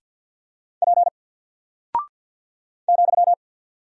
Morse code, heard as a keyed tone, about 48 wpm, 700 Hz. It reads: P ,
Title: Single Character (Letter:Number:Punctuation:Prosign) Alt 50wpm